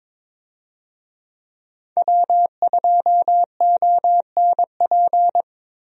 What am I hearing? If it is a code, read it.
Morse code, heard as a keyed tone, about 22 wpm, 700 Hz. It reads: W2ONP
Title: USA Call Signs - 5 Characters - Repeated 3x - 22wpm